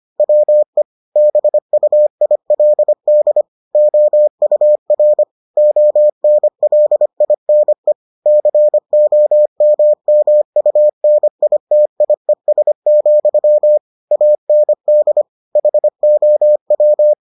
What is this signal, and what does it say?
Morse code, heard as a keyed tone, about 25 wpm, 600 Hz. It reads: WE BUILD OUR ONLINE COMMUNITIES, AND HOW